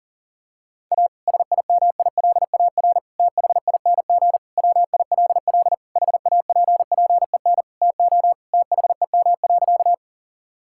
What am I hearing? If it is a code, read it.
Morse code, heard as a keyed tone, about 40 wpm, 700 Hz. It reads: A SIMILAR THING WILL HAPPEN TO THEM.
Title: Sentences from Top 1000 Words - Review Part 5 - 40wpm